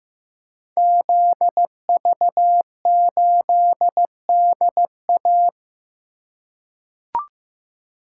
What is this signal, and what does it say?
Morse code, heard as a keyed tone, about 15 wpm, 700 Hz. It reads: ZV8DA
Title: Code Groups - 5 Letter:Numbers - 15wpm